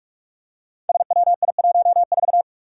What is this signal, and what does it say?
Morse code, heard as a keyed tone, about 44 wpm, 700 Hz. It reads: SWI14
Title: Five Challenging Letter-Number Combinations 45wpm